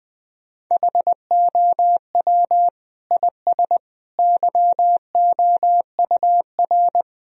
Morse code, heard as a keyed tone, about 20 wpm, 700 Hz. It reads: HOW IS YOUR
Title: Sentences from Top 1000 Words - Review Part 3 - 20wpm